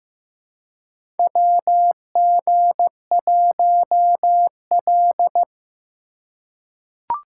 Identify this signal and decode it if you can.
Morse code, heard as a keyed tone, about 15 wpm, 700 Hz. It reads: WG1L
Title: USA Call Signs - 2x Character Spacing - 15wpm